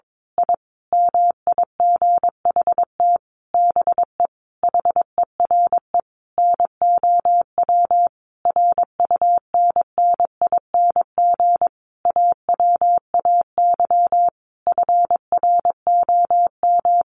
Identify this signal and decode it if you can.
Morse code, heard as a keyed tone, about 22 wpm, 700 Hz. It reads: I MIGHT BE HERE NOW RUNNING AWAY FROM
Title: A Sound Of Thunder 22wpm